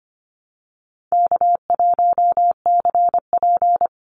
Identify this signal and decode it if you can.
Morse code, heard as a keyed tone, about 25 wpm, 700 Hz. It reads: K1CP
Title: Call Signs Repeated 2x - 25wpm